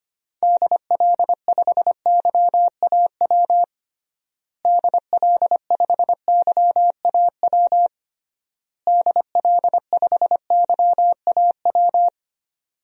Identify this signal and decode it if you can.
Morse code, heard as a keyed tone, about 25 wpm, 700 Hz. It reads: DL5YAW DL5YAW DL5YAW
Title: DXCC-100 Call Signs - 6 Characters - Repeated 3x - 25wpm